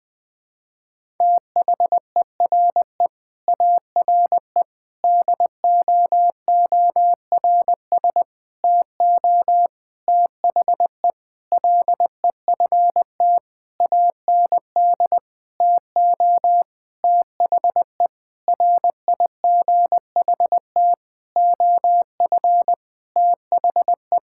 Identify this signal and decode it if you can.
Morse code, heard as a keyed tone, about 20 wpm, 700 Hz. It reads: THERE ARE DOORS TO THE LEFT AND TO THE RIGHT OF THE
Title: Sentences with 8-Word Phrases - 20wpm